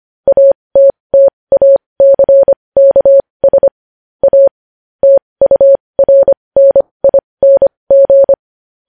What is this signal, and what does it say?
Morse code, heard as a keyed tone, about 25 wpm, spaced as for 20 wpm, 550 Hz. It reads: ATTACKS A TURNING
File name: podcast / News headlines-1x-25x20